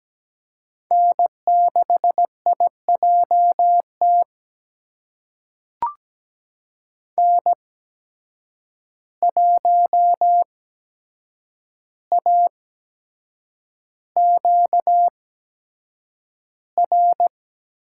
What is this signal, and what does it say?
Morse code, heard as a keyed tone, about 17 wpm, spaced as for 7 wpm, 700 Hz. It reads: N6IJT N1AQR
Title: USA Call Signs - 5 Characters - 8x Character Spacing - 17wpm